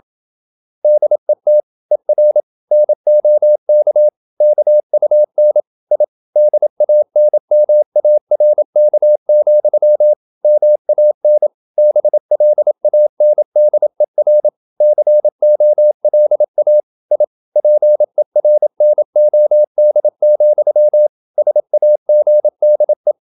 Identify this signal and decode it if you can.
Morse code, heard as a keyed tone, about 27 wpm, 600 Hz. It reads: DET ER NOK KUN I DANMARK, MAN BLANDER COLA I PERNOD, SAGDE
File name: Ferie QRQ 27wpm